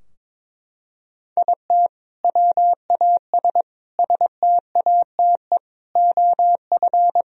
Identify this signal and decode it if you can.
Morse code, heard as a keyed tone, about 21 wpm, 700 Hz. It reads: IT WAS STATE OF
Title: Sentences from Top 1000 Words - Review Part 3 - 22wpm